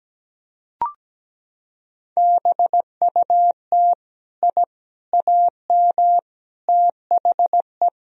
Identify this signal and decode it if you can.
Morse code, heard as a keyed tone, about 17 wpm, 700 Hz. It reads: BUT I AM THE
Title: Sentences from Top 1000 Words - Review Part 4 - 17wpm